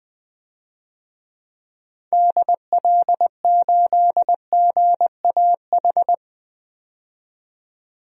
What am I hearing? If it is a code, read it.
Morse code, heard as a keyed tone, about 20 wpm, 700 Hz. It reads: DL8GAH
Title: DXCC-100 Call Signs - 6 Characters - 20wpm